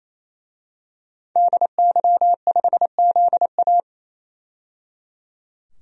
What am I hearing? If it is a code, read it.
Morse code, heard as a keyed tone, about 27 wpm, 700 Hz. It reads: DY5ZA
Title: DXCC-100 Call Signs - 5 Characters - 28wpm